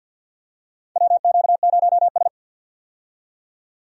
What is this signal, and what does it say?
Morse code, heard as a keyed tone, about 47 wpm, 700 Hz. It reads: WQ0S
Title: USA Call Signs - 4 Characters - 50wpm